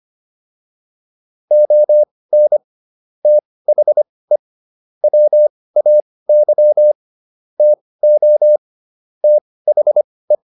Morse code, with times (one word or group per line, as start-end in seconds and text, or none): 1.5-2.6 ON
3.2-4.4 THE
5.0-6.9 WAY
7.6-8.6 TO
9.2-10.3 THE